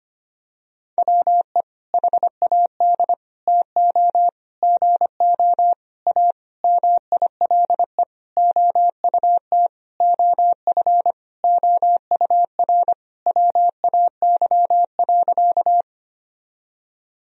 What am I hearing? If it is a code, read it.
Morse code, heard as a keyed tone, about 25 wpm, 700 Hz. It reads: WE HAD TO GO A MILE OUT OF OUR WAY.
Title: Sentences from Top 1000 Words - Review Part 1 - 25wpm